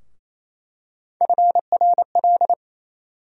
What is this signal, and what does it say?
Morse code, heard as a keyed tone, about 28 wpm, 700 Hz. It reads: FRL
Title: USA Call Sign Postfixes - 3 Letters 28wpm